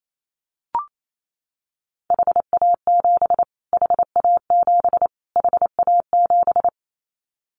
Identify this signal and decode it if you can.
Morse code, heard as a keyed tone, about 28 wpm, 700 Hz. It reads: HA7 HA7 HA7
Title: Callsign Prefixes for Top 25 DXCC - Repeated 3x - 28wpm